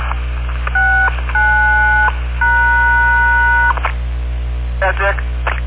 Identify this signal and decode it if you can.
DTMF, heard as telephone touch tones, tones of about 750 ms. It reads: 69#